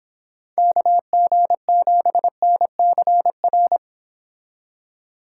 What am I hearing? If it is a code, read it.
Morse code, heard as a keyed tone, about 26 wpm, 700 Hz. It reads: KG7NCR